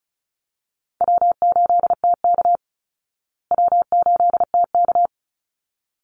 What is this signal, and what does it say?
Morse code, heard as a keyed tone, about 34 wpm, 700 Hz. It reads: W8TK W8TK
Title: Call Signs Repeated 3x - 35wpm